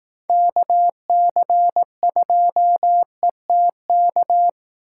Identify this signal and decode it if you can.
Morse code, heard as a keyed tone, about 18 wpm, 700 Hz. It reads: KC2ETK
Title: Speed Racing - USA Call Signs 15wpm